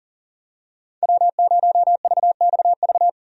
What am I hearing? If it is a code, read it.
Morse code, heard as a keyed tone, about 40 wpm, 700 Hz. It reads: W0VXV
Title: USA Call Signs - 5 Characters - 40wpm